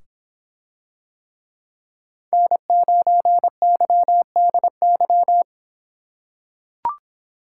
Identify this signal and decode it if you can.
Morse code, heard as a keyed tone, about 24 wpm, 700 Hz. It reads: N9YDY